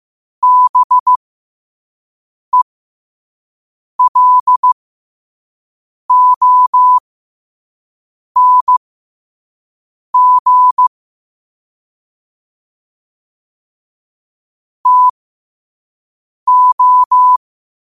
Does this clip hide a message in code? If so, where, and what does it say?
Yes: Morse code from 0.4 to 17.4 s, heard as a keyed tone, about 15 wpm, spaced as for 5 wpm, 1000 Hz. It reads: BELONG TO